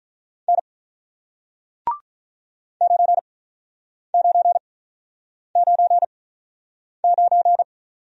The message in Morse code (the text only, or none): N 9999